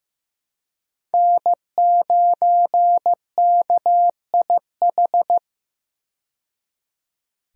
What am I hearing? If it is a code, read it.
Morse code, heard as a keyed tone, about 15 wpm, 700 Hz. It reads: N9KIH
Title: USA Call Signs - 5 Characters - 15wpm